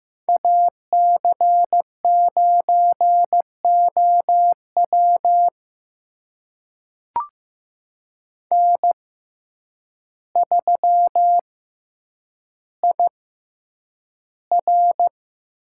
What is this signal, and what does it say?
Morse code, heard as a keyed tone, about 15 wpm, spaced as for 8 wpm, 700 Hz. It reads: AC9OW N3IR
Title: USA Call Signs - 6x Character Spacing - 15wpm